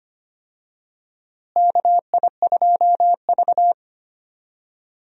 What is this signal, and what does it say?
Morse code, heard as a keyed tone, about 25 wpm, 700 Hz. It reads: KI2V
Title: USA Call Signs - 4 Characters - Repeated 3x - 25wpm